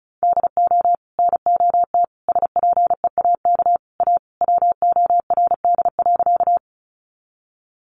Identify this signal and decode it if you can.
Morse code, heard as a keyed tone, about 34 wpm, 700 Hz. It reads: DO NOT SPEAK A WORD.
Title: Sentences from Top 1100 Words 35wpm